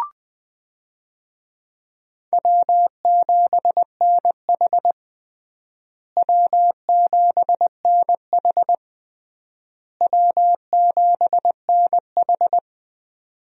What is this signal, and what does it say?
Morse code, heard as a keyed tone, about 21 wpm, 700 Hz. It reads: W7NH W7NH W7NH
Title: USA Call Signs - 4 Characters - Repeated 3x - 20wpm